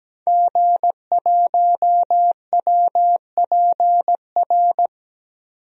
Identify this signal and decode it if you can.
Morse code, heard as a keyed tone, about 17 wpm, 700 Hz. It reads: G1WPR
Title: DXCC-100 Call Signs - 5 Characters - 17wpm